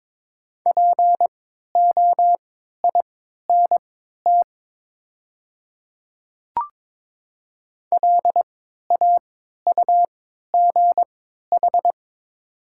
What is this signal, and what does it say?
Morse code, heard as a keyed tone, about 22 wpm, spaced as for 12 wpm, 700 Hz. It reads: POINT LAUGH
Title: 3x Character Spacing - Top 500 Words - 22wpm